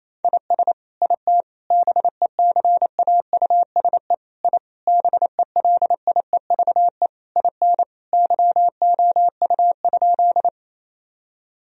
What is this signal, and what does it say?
Morse code, heard as a keyed tone, about 27 wpm, 700 Hz. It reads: IS IT BECAUSE I BELIEVE IN YOU?
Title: Sentences from Top 1000 Words - Review Part 3 - 28wpm